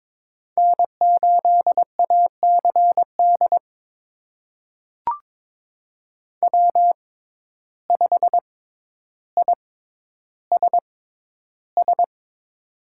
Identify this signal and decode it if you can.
Morse code, heard as a keyed tone, about 22 wpm, spaced as for 11 wpm, 700 Hz. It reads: N8ACD W5ISS
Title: USA Call Signs - 5 Characters - 6x Character Spacing - 22wpm